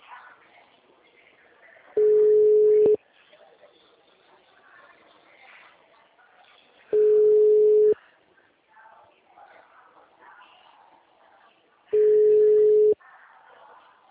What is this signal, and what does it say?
Ringback tone (Europe, 425 Hz), with speech.